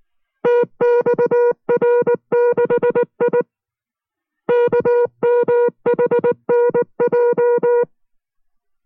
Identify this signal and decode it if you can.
Morse code, heard as a keyed tone, about 19 wpm, 480 Hz. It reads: TXR6I KMHNJ